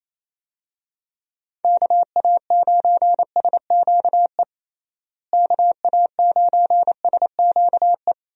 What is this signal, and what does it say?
Morse code, heard as a keyed tone, about 28 wpm, 700 Hz. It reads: KA9SQE KA9SQE
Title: USA Call Signs - 6 Characters - Repeated 2x - 28wpm